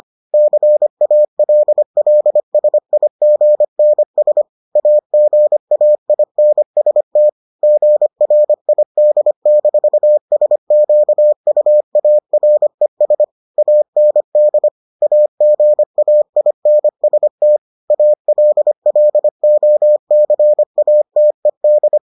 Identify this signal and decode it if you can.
Morse code, heard as a keyed tone, about 25 wpm, 600 Hz. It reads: CALLSIGNS AGAINST GRID-SQUARES AND AGAINST ALLOCATED